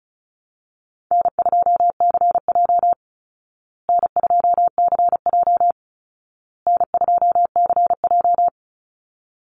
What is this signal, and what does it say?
Morse code, heard as a keyed tone, about 34 wpm, 700 Hz. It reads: N2CJ N2CJ N2CJ
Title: Easy Call Signs Repeated 3x - 35wpm